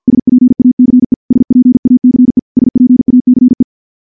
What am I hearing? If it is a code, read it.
Morse code, heard as a keyed tone, about 50 wpm, 250 Hz. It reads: SPACE SPACE SPACE
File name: morse / morse-training-50WPM